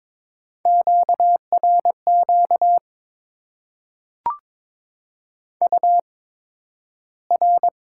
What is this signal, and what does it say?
Morse code, heard as a keyed tone, about 22 wpm, spaced as for 12 wpm, 700 Hz. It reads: QRQ UR
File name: Top 90 QSO Elements - 8x Character Spacing - 22wpm